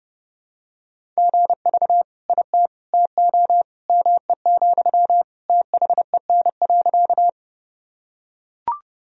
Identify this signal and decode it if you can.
Morse code, heard as a keyed tone, about 30 wpm, 700 Hz. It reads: GV IT TO ME, THEN.
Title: Sentences from Top 200 Words and Abbreviations 30wpm